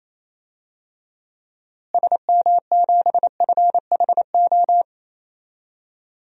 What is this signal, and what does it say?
Morse code, heard as a keyed tone, about 28 wpm, 700 Hz. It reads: SM7FHO
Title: DXCC-100 Call Signs - 6 Characters - Repeated 3x - 28wpm